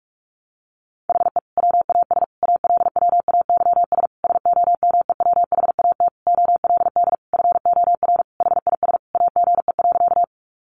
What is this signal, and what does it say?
Morse code, heard as a keyed tone, about 44 wpm, 700 Hz. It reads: HE WAS ALWAYS SOMEWHAT OLD FOR HIS AGE.
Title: Sentences from Top 1300 Words 45wpm